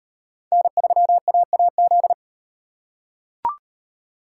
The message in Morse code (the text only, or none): N3AAZ